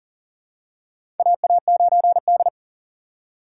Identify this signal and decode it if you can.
Morse code, heard as a keyed tone, about 42 wpm, 690 Hz. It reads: AA9D
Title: USA Call Signs - 4 Characters - 2x Character Spacing - 40wpm